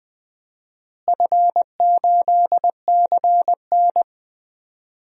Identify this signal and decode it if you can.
Morse code, heard as a keyed tone, about 20 wpm, 700 Hz. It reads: F8CN
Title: DXCC-100 Call Signs - 4 Characters - Repeated 3x - 20wpm